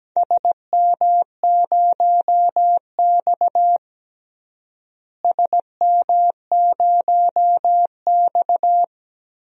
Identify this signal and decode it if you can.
Morse code, heard as a keyed tone, about 17 wpm, 700 Hz. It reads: SM0X SM0X
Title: DXCC-100 Call Signs - 4 Characters - Repeated 3x - 17wpm